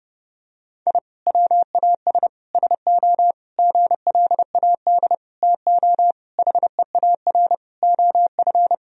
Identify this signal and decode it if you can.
Morse code, heard as a keyed tone, about 30 wpm, 700 Hz. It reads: I WAS SO GLAD TO HEAR OF
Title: Sentences from Top 1000 Words - Review Part 4 - 30wpm